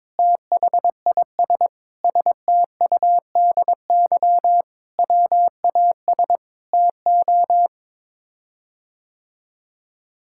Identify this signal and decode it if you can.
Morse code, heard as a keyed tone, about 22 wpm, 700 Hz. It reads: THIS STUDY WAS TO
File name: practice / 22wpm / Sets of 7 Words - 22wpm